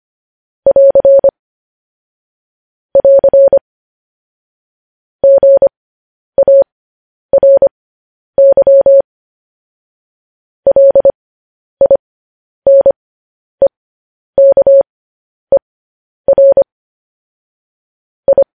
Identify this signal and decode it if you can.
Morse code, heard as a keyed tone, about 25 wpm, spaced as for 10 wpm, 550 Hz. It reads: <AR> <AR> GARY LINEKER I
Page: News Headlines in Morse Code at 10 WPM